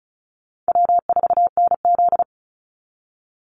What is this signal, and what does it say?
Morse code, heard as a keyed tone, about 34 wpm, 700 Hz. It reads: W4NZ
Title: Call Signs Repeated 2x - 35wpm